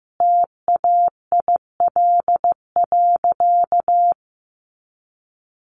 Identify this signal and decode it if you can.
Morse code, heard as a keyed tone, about 15 wpm, 700 Hz. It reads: TAIL.
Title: Easy Phrases 15wpm